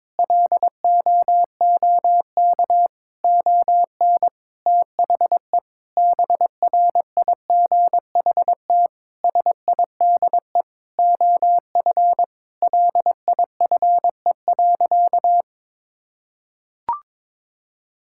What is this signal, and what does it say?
Morse code, heard as a keyed tone, about 22 wpm, 700 Hz. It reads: LOOK ON THE BRIGHT SIDE OF LIFE.
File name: practice / 22wpm / Sentences with 8-Word Phrases - 22wpm